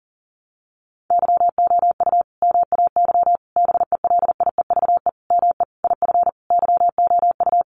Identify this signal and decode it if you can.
Morse code, heard as a keyed tone, about 40 wpm, 700 Hz. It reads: YOU MAY BELIEVE ME IF YOU
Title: Sentences from Top 1000 Words 40wpm